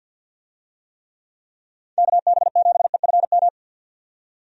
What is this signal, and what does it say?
Morse code, heard as a keyed tone, about 52 wpm, 700 Hz. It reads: KB7EFM
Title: USA Call Signs - 6 Characters - 50wpm